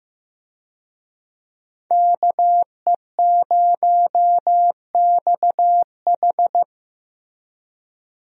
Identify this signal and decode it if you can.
Morse code, heard as a keyed tone, about 15 wpm, 700 Hz. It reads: KE0XH
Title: USA Call Signs - 5 Characters - Repeated 2x - 15wpm